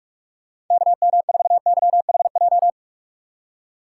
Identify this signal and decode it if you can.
Morse code, heard as a keyed tone, about 42 wpm, 700 Hz. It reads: KM4YHJ
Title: USA Call Signs - 6 Characters - 4x Character Spacing - 45wpm